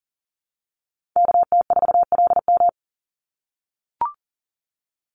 Morse code, heard as a keyed tone, about 40 wpm, 700 Hz. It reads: KT4LM
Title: USA Call Signs 40wpm